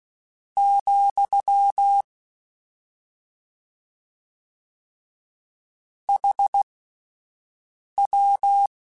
Morse code, heard as a keyed tone, about 16 wpm, spaced as for 5 wpm, 800 Hz. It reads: , HW
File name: QSOs / QSOs-mp3s / WD0BGP 1@13